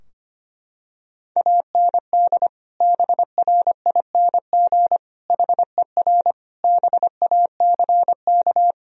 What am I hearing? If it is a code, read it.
Morse code, heard as a keyed tone, about 25 wpm, 700 Hz. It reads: AND BRING HER BACK
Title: Sentences from Top 1000 Words - Review Part 3 - 25wpm